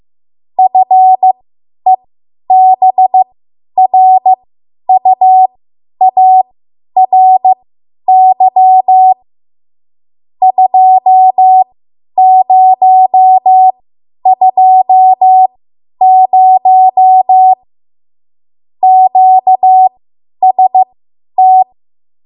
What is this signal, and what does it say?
Morse code, heard as a keyed tone, about 15 wpm, spaced as for 10 wpm, 750 Hz. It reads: FEBRUARY 2020 QST
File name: Archive / 10 WPM / 10WPM